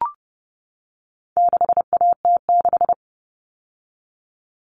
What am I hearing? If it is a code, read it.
Morse code, heard as a keyed tone, about 30 wpm, 700 Hz. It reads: 6AT6
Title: All Cut Numbers - 4-Digit 30wpm